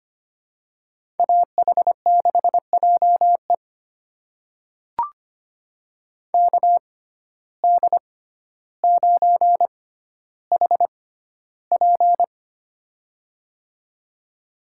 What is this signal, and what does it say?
Morse code, heard as a keyed tone, about 25 wpm, spaced as for 13 wpm, 690 Hz. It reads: AH6JE KD9HP
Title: USA Call Signs - 5 Characters - 6x Character Spacing - 25wpm